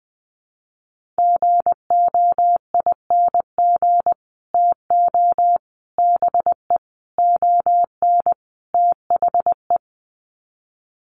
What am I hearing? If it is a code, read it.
Morse code, heard as a keyed tone, about 20 wpm, 700 Hz. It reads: GOING TO BE ON THE
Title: Sets of 5 Words Encore2 20wpm